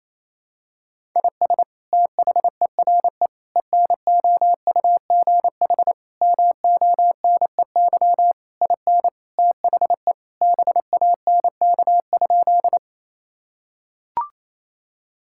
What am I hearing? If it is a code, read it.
Morse code, heard as a keyed tone, about 28 wpm, 700 Hz. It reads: IS THERE ENOUGH MONEY IN THE BANK?
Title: Sentences from Top 1000 Words - Review Part 6 - 28wpm